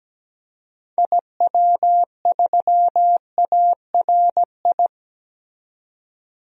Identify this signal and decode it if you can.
Morse code, heard as a keyed tone, about 17 wpm, 700 Hz. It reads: IW3ARI
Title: DXCC-100 Call Signs - 6 Characters - 17wpm